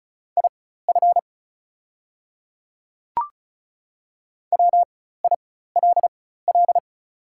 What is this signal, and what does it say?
Morse code, heard as a keyed tone, about 34 wpm, spaced as for 16 wpm, 700 Hz. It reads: IF WILL
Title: 4x Character Spacing - Top 100 Words - 35wpm